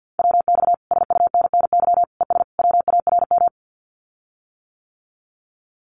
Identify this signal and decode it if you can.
Morse code, heard as a keyed tone, about 50 wpm, 700 Hz. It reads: WX SUNNY ES WARM
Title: Common QSO Phrases Encore 50wpm